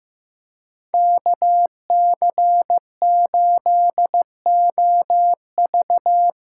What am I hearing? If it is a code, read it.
Morse code, heard as a keyed tone, about 15 wpm, 700 Hz. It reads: KC8OV
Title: USA Call Signs - 6x Character Spacing - 15wpm